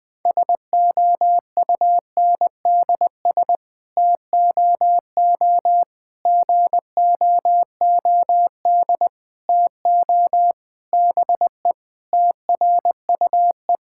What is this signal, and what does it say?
Morse code, heard as a keyed tone, about 20 wpm, 700 Hz. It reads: SOUNDS TOO GOOD TO BE TRUE
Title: Sets of 9 Words - 20wpm